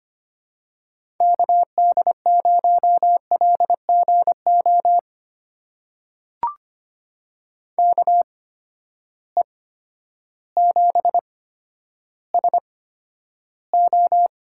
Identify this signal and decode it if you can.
Morse code, heard as a keyed tone, about 25 wpm, spaced as for 11 wpm, 700 Hz. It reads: KD0LGO KE7SO
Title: USA Call Signs - 6 Characters - 8x Character Spacing - 25wpm